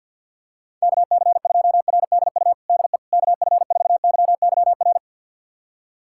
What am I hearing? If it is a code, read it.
Morse code, heard as a keyed tone, about 52 wpm, 700 Hz. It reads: KC2RDU DE KF4YYR
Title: USA Callsign DE Callsign 50wpm